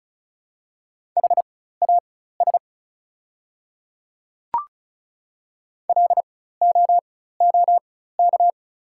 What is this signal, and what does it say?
Morse code, heard as a keyed tone, about 36 wpm, spaced as for 16 wpm, 690 Hz. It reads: HAS LOOK
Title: 4x Character Spacing - Top 100 Words - 35wpm